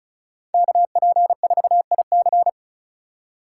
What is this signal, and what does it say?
Morse code, heard as a keyed tone, about 44 wpm, 700 Hz. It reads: KP4IC